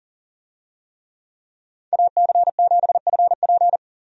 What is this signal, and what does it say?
Morse code, heard as a keyed tone, about 40 wpm, 700 Hz. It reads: AC7FP
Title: USA Call Signs - 5 Characters - 40wpm